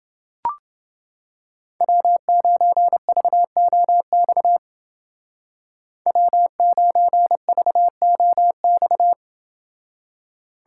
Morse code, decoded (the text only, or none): W9VOX W9VOX